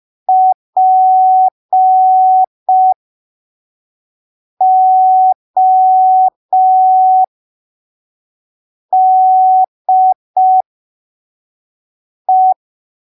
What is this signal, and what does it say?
Morse code, heard as a keyed tone, about 5 wpm, spaced as for 3 wpm, 750 Hz. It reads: PODE
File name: arquivos / texto / texto 02 5ppm